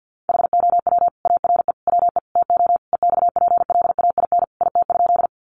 Morse code, heard as a keyed tone, about 50 wpm, 700 Hz. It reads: HOW ARE WE TO EXPLAIN IT?
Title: Sentences from Top 1100 Words 50wpm